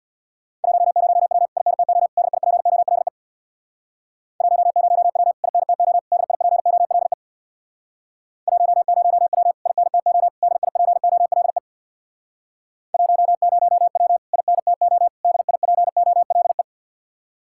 Morse code, heard as a keyed tone, about 66 wpm, 700 Hz. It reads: 10W INTO DIPOLE 10W INTO DIPOLE 10W INTO DIPOLE 10W INTO DIPOLE